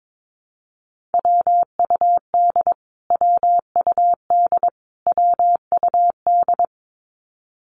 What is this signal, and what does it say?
Morse code, heard as a keyed tone, about 22 wpm, 700 Hz. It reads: WUD WUD WUD
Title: Abbreviations Repeated 3x - 22wpm